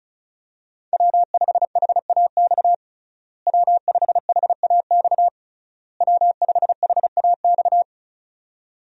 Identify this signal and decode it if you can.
Morse code, heard as a keyed tone, about 36 wpm, 700 Hz. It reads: W5HAX W5HAX W5HAX
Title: USA Call Signs - 5 Characters - Repeated 3x - 35wpm